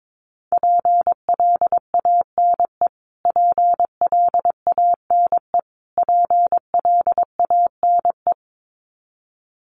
Morse code, heard as a keyed tone, about 22 wpm, 700 Hz. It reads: PLANE PLANE PLANE
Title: Top 500 Words Repeated 22wpm